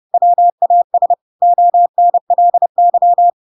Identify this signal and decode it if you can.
Morse code, heard as a keyed tone, about 30 wpm, 700 Hz. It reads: WAS ONLY
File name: Pinocchio / Pinocchio ch5 30 30 ik1hge